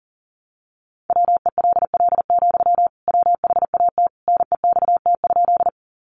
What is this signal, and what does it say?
Morse code, heard as a keyed tone, about 40 wpm, 700 Hz. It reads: WELL, WHAT NEXT?
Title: Sentences from Top 300 Words 40wpm